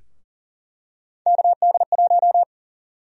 Morse code, heard as a keyed tone, about 40 wpm, 700 Hz. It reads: KD1
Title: USA Call Sign Prefixes - 40wpm